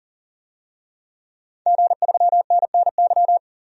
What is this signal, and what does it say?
Morse code, heard as a keyed tone, about 40 wpm, 700 Hz. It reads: G3NNY
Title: DXCC-100 Call Signs - 5 Characters - 40wpm